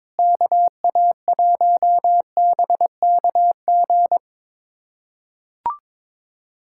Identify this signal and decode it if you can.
Morse code, heard as a keyed tone, about 22 wpm, 700 Hz. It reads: KA1BKG
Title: USA Callsign DE Callsign - 6 Characters - 22wpm